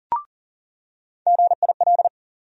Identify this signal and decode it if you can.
Morse code, heard as a keyed tone, about 40 wpm, 700 Hz. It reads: GIL
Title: USA Call Sign Postfixes - 3 Letters 40wpm